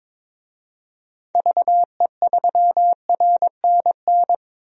Morse code, heard as a keyed tone, about 22 wpm, 700 Hz. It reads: VE3RNN
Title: DXCC-100 Call Signs - 6 Characters - Repeated 2x - 22wpm